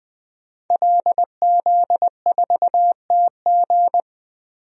Morse code, heard as a keyed tone, about 20 wpm, 700 Hz. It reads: LZ4TG
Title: DXCC-100 Call Signs - 5 Characters - Repeated 2x - 20wpm